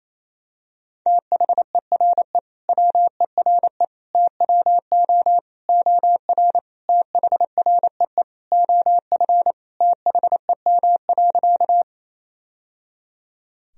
Morse code, heard as a keyed tone, about 28 wpm, 700 Hz. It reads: THERE WERE TWO OR THREE OF THEM.
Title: Sentences from Top 1000 Words - Review Part 1 - 28wpm